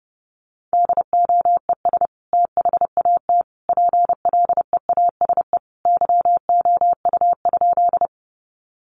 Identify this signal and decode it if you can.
Morse code, heard as a keyed tone, about 30 wpm, 700 Hz. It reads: DOES THAT PLEASE YOU?
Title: Sentences from Top 1500 Words 30wpm